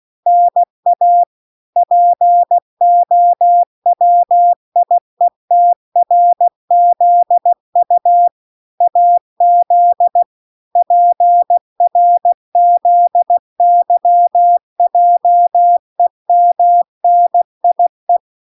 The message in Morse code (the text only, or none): NA POWIETRZU AZ PRZYJEMNIE